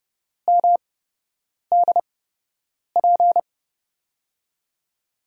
Morse code, heard as a keyed tone, about 31 wpm, spaced as for 8 wpm, 700 Hz. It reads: MDP
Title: USA Call Signs - 5 Characters - 8x Character Spacing - 30wpm